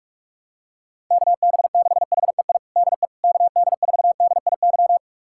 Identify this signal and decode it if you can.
Morse code, heard as a keyed tone, about 46 wpm, 700 Hz. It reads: KB6HEI DE KD4DIY
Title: USA Callsign DE Callsign - 6 Characters - 45wpm